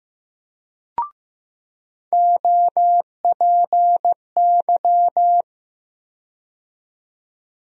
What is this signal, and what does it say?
Morse code, heard as a keyed tone, about 15 wpm, 700 Hz. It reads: OPY
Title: USA Call Sign Postfixes - 3 Letters 15wpm